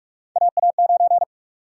Morse code, heard as a keyed tone, about 46 wpm, 700 Hz. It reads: AA9
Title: USA Call Sign Prefixes - 45wpm